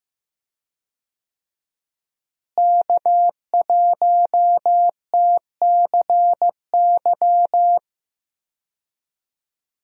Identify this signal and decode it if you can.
Morse code, heard as a keyed tone, about 16 wpm, 700 Hz. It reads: K1TCY
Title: USA Call Signs - 5 Characters - Repeated 3x - 15wpm